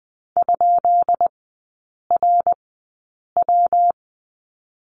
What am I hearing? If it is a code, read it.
Morse code, heard as a keyed tone, about 20 wpm, spaced as for 8 wpm, 700 Hz. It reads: ?RW
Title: 133 Lesson 22e - Question Mark - Three Characters 20wpm